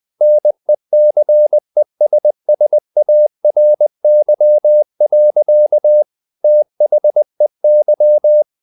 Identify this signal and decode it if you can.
Morse code, heard as a keyed tone, about 20 wpm, 600 Hz. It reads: NECESSARY. THEY
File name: the old man and the sea / the old man and the sea part 1 20wpm